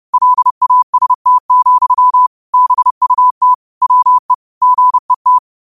Morse code, heard as a keyed tone, about 30 wpm, 1000 Hz. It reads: LAST, BUT WE GET
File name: qotdfiles / 30wpm